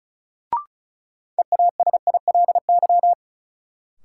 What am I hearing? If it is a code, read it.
Morse code, heard as a keyed tone, about 34 wpm, 700 Hz. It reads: EASILY
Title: Sentences from Top 900 Words - New Word Drill - 35wpm